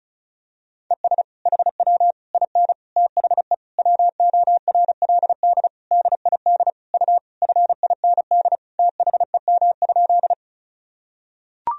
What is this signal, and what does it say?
Morse code, heard as a keyed tone, about 36 wpm, 700 Hz. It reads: ES HW IN THE WORLD DID U FIND THEM?
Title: Sentences from Top 200 Words and Abbreviations 35wpm